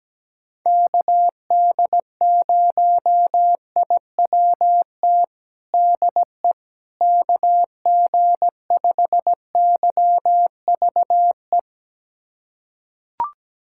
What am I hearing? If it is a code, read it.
Morse code, heard as a keyed tone, about 17 wpm, 700 Hz. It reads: KD0IWT DE KG5YVE
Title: USA Callsign DE Callsign - 6 Characters - 17wpm